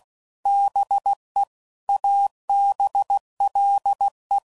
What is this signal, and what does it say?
Morse code, heard as a keyed tone, about 16 wpm, 800 Hz. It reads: BE ABLE
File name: CW Code Groups @15WPM / Technical Subjects / OR Gate